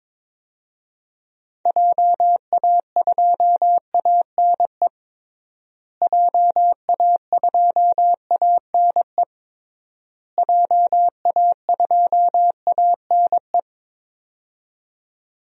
Morse code, heard as a keyed tone, about 22 wpm, 700 Hz. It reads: JA2ANE JA2ANE JA2ANE
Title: DXCC-100 Call Signs - 6 Characters - Repeated 3x - 22wpm